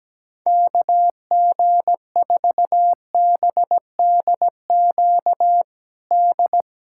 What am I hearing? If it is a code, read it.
Morse code, heard as a keyed tone, about 17 wpm, 700 Hz. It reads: KG4BDQ D
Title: USA Callsign DE Callsign 17wpm